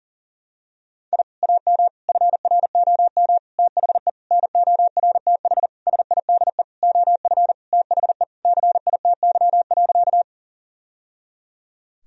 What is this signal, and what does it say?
Morse code, heard as a keyed tone, about 40 wpm, 700 Hz. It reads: I AM FROM THE NORTH SIDE OF THE CITY.
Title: Sentences from Top 1000 Words - Review Part 1 - 40wpm